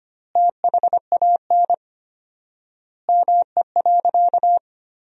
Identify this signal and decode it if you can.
Morse code, heard as a keyed tone, about 25 wpm, 700 Hz. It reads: THAN ME.
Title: 4x Word-Spacing - Sentences from Top 400 Words - 25wpm